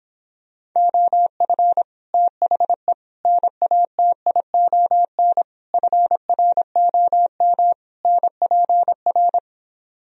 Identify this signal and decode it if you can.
Morse code, heard as a keyed tone, about 26 wpm, 700 Hz. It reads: OF THE NATION FROM NPR